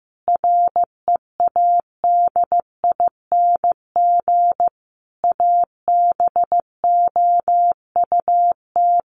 Morse code, heard as a keyed tone, about 15 wpm, 700 Hz. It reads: READING ABOUT
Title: Sentences from Top 600 Words 15wpm